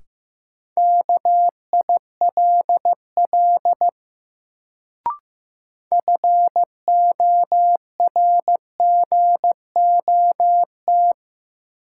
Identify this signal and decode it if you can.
Morse code, heard as a keyed tone, about 15 wpm, 700 Hz. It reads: KILL FORGOT
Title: Sentences from Top 1400 Words - New Word Drill Alt - 15wpm